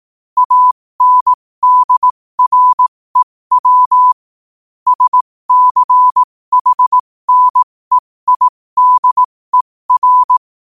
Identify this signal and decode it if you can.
Morse code, heard as a keyed tone, about 18 wpm, spaced as for 15 wpm, 1000 Hz. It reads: ANDREW SCHNEIDER
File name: qotdfiles / 15wpm